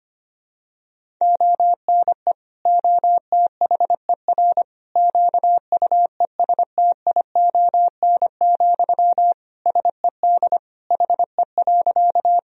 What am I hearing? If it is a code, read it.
Morse code, heard as a keyed tone, about 25 wpm, 700 Hz. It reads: ONE OTHER QUESTION, SED HE.
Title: Sentences from Top 400 Words and Abbreviations 25wpm